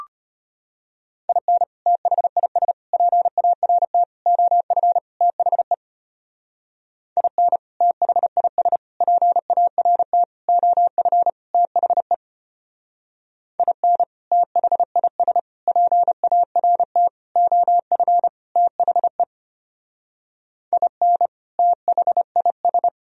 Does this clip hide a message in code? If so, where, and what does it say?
Yes: Morse code from 1.3 to 22.9 s, heard as a keyed tone, about 32 wpm, 700 Hz. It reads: IN THIS PART OF THE IN THIS PART OF THE IN THIS PART OF THE IN THIS